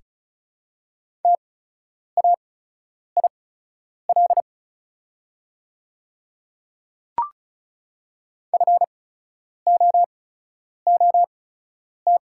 Morse code, heard as a keyed tone, about 36 wpm, spaced as for 10 wpm, 690 Hz. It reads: TAIL FOOT
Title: 8x Character Spacing - Top 500 Words - 35wpm